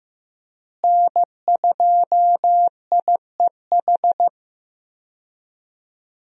Morse code, heard as a keyed tone, about 15 wpm, 700 Hz. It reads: N2IEH
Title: USA Call Signs - 5 Characters - Repeated 3x - 15wpm